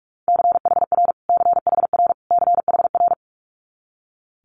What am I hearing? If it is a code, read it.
Morse code, heard as a keyed tone, about 46 wpm, 700 Hz. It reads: CHR CHR CHR
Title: Multi-letter Phonemes Repeated 3x 45wpm